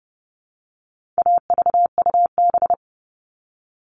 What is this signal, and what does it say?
Morse code, heard as a keyed tone, about 30 wpm, 700 Hz. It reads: AVUB
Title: All Cut Numbers - 4-Digit 30wpm